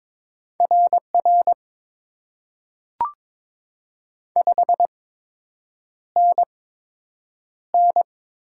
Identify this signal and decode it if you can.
Morse code, heard as a keyed tone, about 22 wpm, spaced as for 8 wpm, 700 Hz. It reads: RR 5NN
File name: Top 90 QSO Elements - 8x Character Spacing - 22wpm